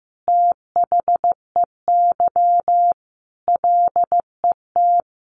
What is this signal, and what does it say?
Morse code, heard as a keyed tone, about 15 wpm, 700 Hz. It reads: THEY LET
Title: Sentences from Top 1500 Words 15wpm